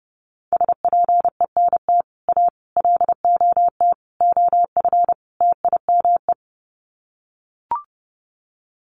Morse code, heard as a keyed tone, about 30 wpm, 700 Hz. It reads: SPENT A LOT OF TIME
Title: Sets of 5 Words 30wpm